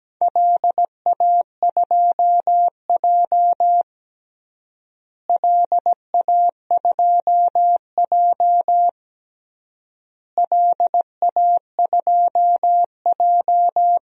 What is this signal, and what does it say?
Morse code, heard as a keyed tone, about 17 wpm, 700 Hz. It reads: LA2J LA2J LA2J
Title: DXCC-100 Call Signs - 4 Characters - Repeated 3x - 17wpm